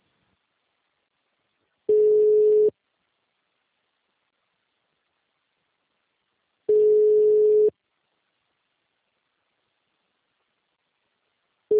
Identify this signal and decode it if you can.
Ringback tone (Europe, 425 Hz).
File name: out- -2024- - - .35